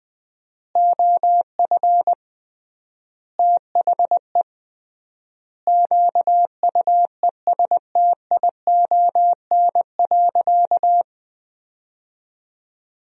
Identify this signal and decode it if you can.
Morse code, heard as a keyed tone, about 21 wpm, 700 Hz. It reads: OF THE QUESTION.
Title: 3x Word-Spacing - Sentences from Top 400 Words - 20wpm